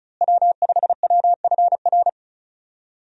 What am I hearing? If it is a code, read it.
Morse code, heard as a keyed tone, about 36 wpm, 700 Hz. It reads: W5WFR
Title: USA Call Signs - 5 Characters - Repeated 3x - 35wpm